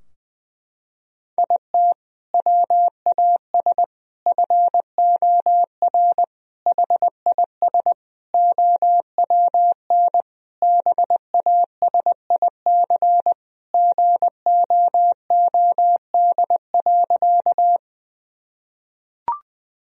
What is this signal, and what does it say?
Morse code, heard as a keyed tone, about 21 wpm, 700 Hz. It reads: IT WAS FOR HIS OWN BASIC GOOD.
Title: Sentences from Top 1000 Words - Review Part 3 - 20wpm